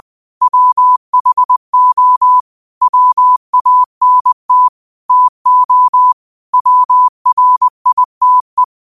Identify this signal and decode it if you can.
Morse code, heard as a keyed tone, about 20 wpm, 1000 Hz. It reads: WHO WANT TO WRITE
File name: qotdfiles / 20wpm